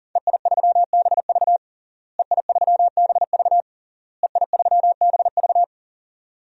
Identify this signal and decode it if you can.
Morse code, heard as a keyed tone, about 40 wpm, 700 Hz. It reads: EI3BV EI3BV EI3BV
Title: DXCC-100 Call Signs - 5 Characters - Repeated 3x - 40wpm